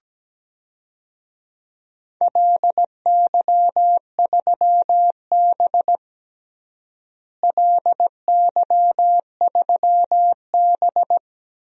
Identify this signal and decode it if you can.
Morse code, heard as a keyed tone, about 17 wpm, 700 Hz. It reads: LY3B LY3B
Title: DXCC-100 Call Signs - 4 Characters - Repeated 3x - 17wpm